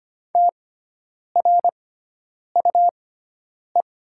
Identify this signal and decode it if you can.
Morse code, heard as a keyed tone, about 25 wpm, spaced as for 9 wpm, 700 Hz. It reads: TRUE
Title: 6x Character Spacing - Top 500 Words - 25wpm